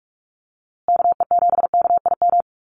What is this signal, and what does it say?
Morse code, heard as a keyed tone, about 44 wpm, 700 Hz. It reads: KE7KIM
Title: USA Call Signs Repeated 3x 45wpm